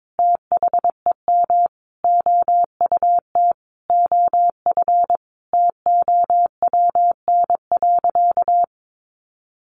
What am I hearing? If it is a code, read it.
Morse code, heard as a keyed tone, about 22 wpm, 700 Hz. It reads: THEM OUT OF TOWN.
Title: Sentences from Top 1300 Words 22wpm